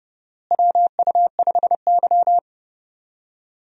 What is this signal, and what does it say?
Morse code, heard as a keyed tone, about 29 wpm, 700 Hz. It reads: WU5Y
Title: USA Call Signs - 4 Characters - Repeated 3x - 30wpm